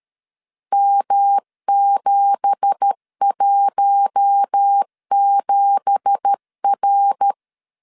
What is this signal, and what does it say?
Morse code, heard as a keyed tone, about 13 wpm, 800 Hz. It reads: M717R
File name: CW / DET / 1 9 1500C S A 12wpm 3800